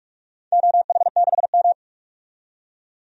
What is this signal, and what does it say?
Morse code, heard as a keyed tone, about 44 wpm, 700 Hz. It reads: OH6M
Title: DXCC-100 Call Signs - 4 Characters - 45wpm